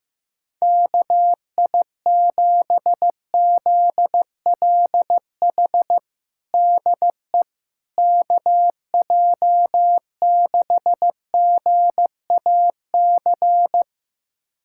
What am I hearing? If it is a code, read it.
Morse code, heard as a keyed tone, about 15 wpm, 700 Hz. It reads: KI7ZLH DE KJ6GAC
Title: USA Callsign DE Callsign - 6 Characters - 15wpm